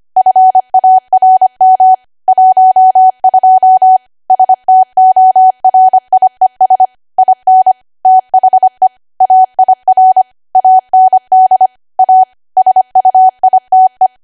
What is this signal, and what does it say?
Morse code, heard as a keyed tone, about 25 wpm, 750 Hz. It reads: FARM 12 STORIES IN THE AIR AND A SUITE
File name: Archive / 25 WPM / 25WPM